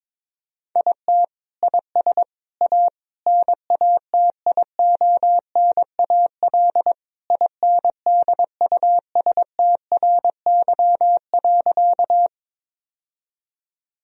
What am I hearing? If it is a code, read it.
Morse code, heard as a keyed tone, about 22 wpm, 700 Hz. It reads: IT IS A NATIONAL INDUSTRY.
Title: Sentences from Top 1000 Words - Review Part 4 - 22wpm